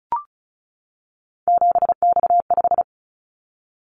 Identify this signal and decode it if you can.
Morse code, heard as a keyed tone, about 34 wpm, 700 Hz. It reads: 7X5
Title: Callsign Prefixes for Top 100 DXCC - 35wpm